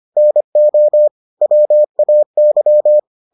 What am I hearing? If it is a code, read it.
Morse code, heard as a keyed tone, about 25 wpm, 600 Hz. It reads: NO WAY